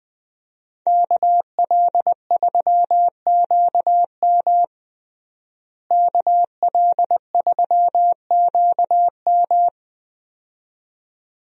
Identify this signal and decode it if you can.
Morse code, heard as a keyed tone, about 20 wpm, 700 Hz. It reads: KL3QM KL3QM
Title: USA Call Signs - 5 Characters - Repeated 2x - 20wpm